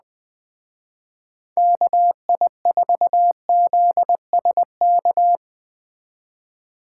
Morse code, heard as a keyed tone, about 20 wpm, 700 Hz. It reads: KI4ZSK
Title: Speed Racing - USA 2x3 Call Signs 17wpm